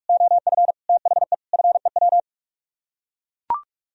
Morse code, heard as a keyed tone, about 44 wpm, 700 Hz. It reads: OF THE FEW